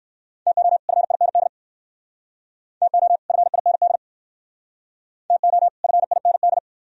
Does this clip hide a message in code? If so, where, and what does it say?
Yes: Morse code from 0.5 to 6.6 s, heard as a keyed tone, about 59 wpm, 700 Hz. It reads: TO FIND TO FIND TO FIND